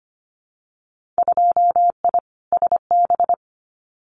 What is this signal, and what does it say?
Morse code, heard as a keyed tone, about 25 wpm, 700 Hz. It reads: 2I SB
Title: Winter Field Day Exchanges Encore 25wpm